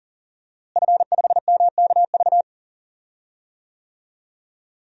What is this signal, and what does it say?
Morse code, heard as a keyed tone, about 37 wpm, 700 Hz. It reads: F5MKV
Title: DXCC-100 Call Signs - 5 Characters - Repeated 3x - 40wpm